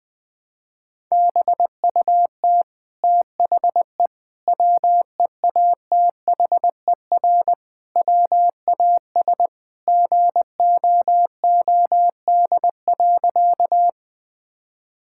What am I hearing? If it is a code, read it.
Morse code, heard as a keyed tone, about 20 wpm, 700 Hz. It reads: BUT THE WEATHER WAS GOOD.
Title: Sentences from Top 1000 Words - Review Part 5 - 20wpm